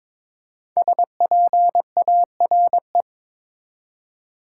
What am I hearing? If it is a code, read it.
Morse code, heard as a keyed tone, about 22 wpm, 700 Hz. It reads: SPARE
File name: 5 Letter Words - First 3 Letters Given - 22wpm